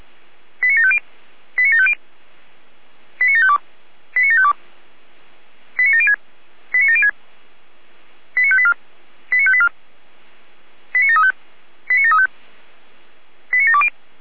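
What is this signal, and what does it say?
Five-tone selective call (ZVEI), heard as quick runs of five tones, tones of about 70 ms. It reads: 89750 89750 89752 89752 89706 89706 89584 89584 89735 89735 89720
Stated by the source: Received on an SDR, ZVEI five-tone selective call, channel 455.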